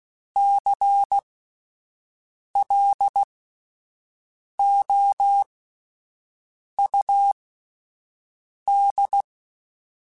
Morse code, heard as a keyed tone, about 16 wpm, spaced as for 5 wpm, 800 Hz. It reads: CLOUD
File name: Misc / Misc-mp3s / WordPairs sp @10